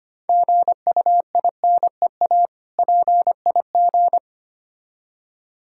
Morse code, heard as a keyed tone, about 25 wpm, 700 Hz. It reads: GUINEA PIG
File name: practice / 25wpm / Difficult Words with Hints - 25wpm